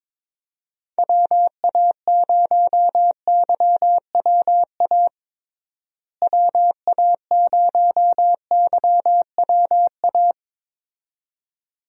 Morse code, heard as a keyed tone, about 22 wpm, 700 Hz. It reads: WA0YWA WA0YWA
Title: USA Call Signs - 6 Characters - Repeated 2x - 22wpm